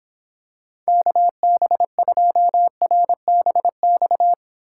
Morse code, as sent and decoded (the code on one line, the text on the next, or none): -.- -... ..--- .-. -... -..-
KB2RBX